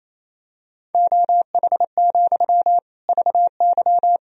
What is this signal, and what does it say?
Morse code, heard as a keyed tone, about 28 wpm, 700 Hz. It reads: OH, VY
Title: Sentences from Top 500 Words and Abbreviations 28wpm